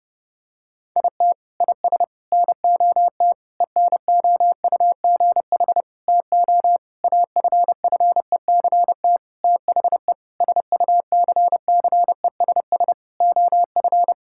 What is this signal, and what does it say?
Morse code, heard as a keyed tone, about 32 wpm, 700 Hz. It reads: IT IS NOT ENOUGH TO AFFECT THE SUCCESS OF
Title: Sentences from Top 1000 Words - Review Part 6 - 30wpm